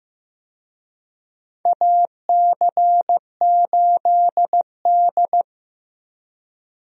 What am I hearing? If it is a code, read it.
Morse code, heard as a keyed tone, about 15 wpm, 700 Hz. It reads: AC8D
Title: USA Call Signs - 4 Characters - Repeated 3x - 15wpm